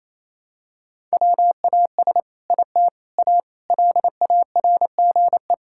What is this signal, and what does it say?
Morse code, heard as a keyed tone, about 28 wpm, 700 Hz. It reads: WAS IT A LARGE
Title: Sentences from Top 1000 Words - Review Part 5 - 28wpm